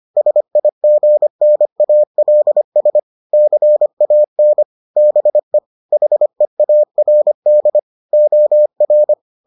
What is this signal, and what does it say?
Morse code, heard as a keyed tone, about 25 wpm, 600 Hz. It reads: SIGNALS CAN BE HEARD OR